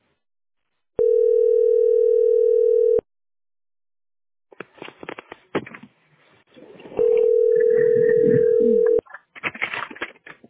Ringback tone (North America), 440 + 480 Hz.